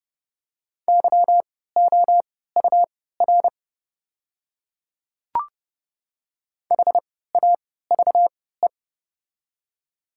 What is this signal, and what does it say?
Morse code, heard as a keyed tone, about 30 wpm, spaced as for 17 wpm, 700 Hz. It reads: YOUR HAVE
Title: 3x Character Spacing - Top 100 Words - 30wpm